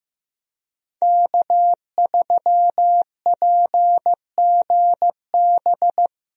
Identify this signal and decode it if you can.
Morse code, heard as a keyed tone, about 15 wpm, 700 Hz. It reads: K3PGB
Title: USA Call Signs - 5 Characters - 15wpm